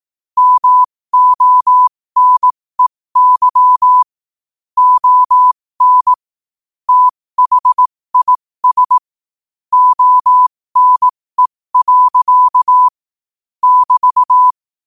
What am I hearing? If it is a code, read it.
Morse code, heard as a keyed tone, about 18 wpm, spaced as for 15 wpm, 1000 Hz. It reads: MONEY ON THIS ONE. <BT>
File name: qotdfiles / 15wpm